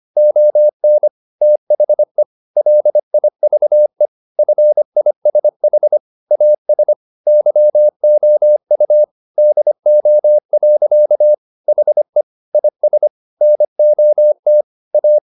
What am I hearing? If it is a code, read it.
Morse code, heard as a keyed tone, about 25 wpm, 600 Hz. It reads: ON THE LIVE FISH AS YOU DO. HE IS NOT A